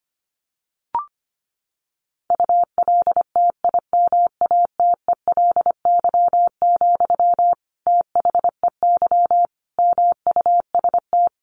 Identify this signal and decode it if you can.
Morse code, heard as a keyed tone, about 25 wpm, 700 Hz. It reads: ULTIMATELY, THEY MUST
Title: Sentences from Top 1200 Words 25wpm